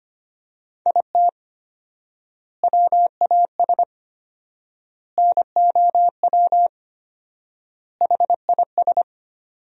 Morse code, heard as a keyed tone, about 25 wpm, 700 Hz. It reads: IT WAS NOW HIS
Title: 4x Word-Spacing - Sentences from Top 200 Words - 25wpm